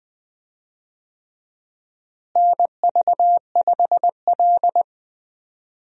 Morse code, heard as a keyed tone, about 20 wpm, 700 Hz. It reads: NV5L
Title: USA Call Signs - 4 Characters - Repeated 3x - 20wpm